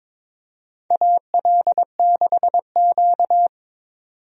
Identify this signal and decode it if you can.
Morse code, heard as a keyed tone, about 22 wpm, 700 Hz. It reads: AL6Q
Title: USA Call Signs - 4 Characters - 22wpm